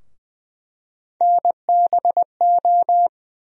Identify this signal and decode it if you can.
Morse code, heard as a keyed tone, about 21 wpm, 700 Hz. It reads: NBO
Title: USA Call Sign Postfixes - 3 Letters 20wpm